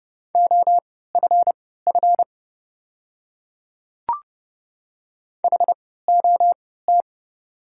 Morse code, heard as a keyed tone, about 31 wpm, spaced as for 17 wpm, 700 Hz. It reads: OFF HOT
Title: 3x Character Spacing - Top 300 Words - 30wpm